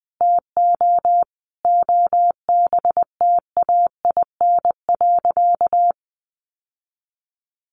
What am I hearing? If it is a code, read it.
Morse code, heard as a keyed tone, about 20 wpm, 700 Hz. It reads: TO OBTAIN.
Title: Sentences from Top 1300 Words 20wpm